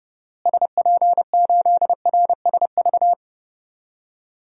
Morse code, heard as a keyed tone, about 30 wpm, 700 Hz. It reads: SP8RSV
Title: DXCC-100 Call Signs - 6 Characters - Repeated 2x - 30wpm